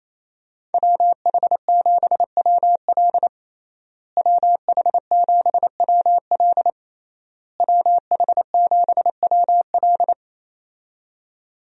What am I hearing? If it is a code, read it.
Morse code, heard as a keyed tone, about 28 wpm, 700 Hz. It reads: WH7WL WH7WL WH7WL
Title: USA Call Signs - 5 Characters - Repeated 3x - 28wpm